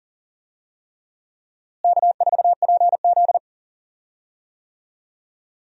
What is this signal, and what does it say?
Morse code, heard as a keyed tone, about 40 wpm, 700 Hz. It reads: K4PZ
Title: USA Call Signs - 4 Characters - Repeated 3x - 40wpm